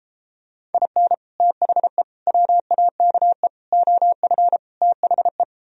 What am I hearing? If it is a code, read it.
Morse code, heard as a keyed tone, about 33 wpm, 700 Hz. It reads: IN THE WAKE OF THE